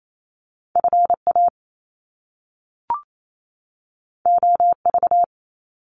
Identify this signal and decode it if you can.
Morse code, heard as a keyed tone, about 28 wpm, 700 Hz. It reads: FA OV
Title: Top Two Letter Combinations in Words 28wpm